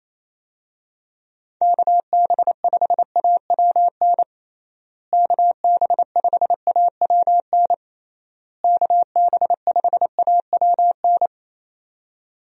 Morse code, heard as a keyed tone, about 28 wpm, 700 Hz. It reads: KB5AWN KB5AWN KB5AWN
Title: USA Call Signs - 6 Characters - Repeated 3x - 28wpm